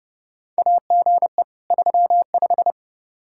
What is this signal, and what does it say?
Morse code, heard as a keyed tone, about 30 wpm, 700 Hz. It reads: AGE 35
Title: Practice QSO - Exchange 3 - More Info - 30wpm